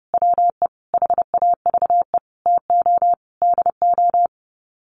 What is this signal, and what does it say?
Morse code, heard as a keyed tone, about 30 wpm, 700 Hz. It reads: WE HAVE TO DO
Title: Sets of 5 Words Encore 30wpm